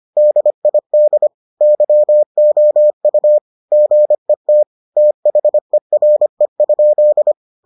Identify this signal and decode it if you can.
Morse code, heard as a keyed tone, about 25 wpm, 600 Hz. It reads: DID YOU GET THERE?